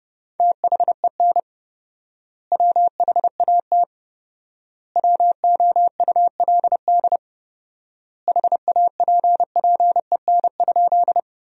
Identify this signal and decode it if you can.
Morse code, heard as a keyed tone, about 30 wpm, 700 Hz. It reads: THEN WHAT WOULD HAPPEN?
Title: 4x Word-Spacing - Sentences from Top 400 Words - 30wpm